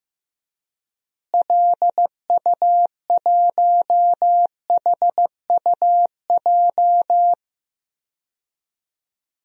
Morse code, heard as a keyed tone, about 15 wpm, 700 Hz. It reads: LU1HUJ
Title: DXCC-100 Call Signs - 6 Characters - Repeated 3x - 15wpm